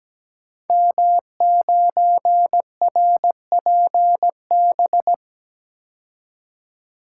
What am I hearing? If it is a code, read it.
Morse code, heard as a keyed tone, about 17 wpm, 700 Hz. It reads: M9RPB
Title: Code Groups - 5 Letter:Numbers - 17wpm